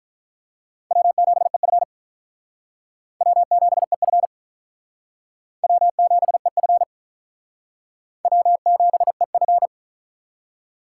Morse code, heard as a keyed tone, about 46 wpm, 700 Hz. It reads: W7EF W7EF W7EF W7EF